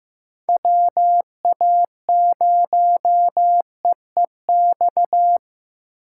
Morse code, heard as a keyed tone, about 15 wpm, 700 Hz. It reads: WA0EEX
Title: USA Call Signs - 8x Character Spacing - 15wpm